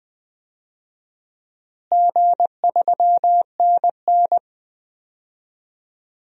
Morse code, heard as a keyed tone, about 20 wpm, 700 Hz. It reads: G3NN
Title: DXCC-100 Call Signs - 4 Characters - Repeated 2x - 20wpm